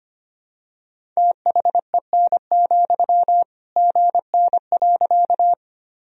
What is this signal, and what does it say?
Morse code, heard as a keyed tone, about 25 wpm, 700 Hz. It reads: THEN, GN.
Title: Sentences from Top 300 Words and Abbreviations 25wpm